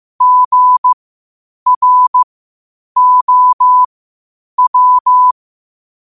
Morse code, heard as a keyed tone, about 15 wpm, spaced as for 8 wpm, 1000 Hz. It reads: GROW
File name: qotdfiles / 8wpm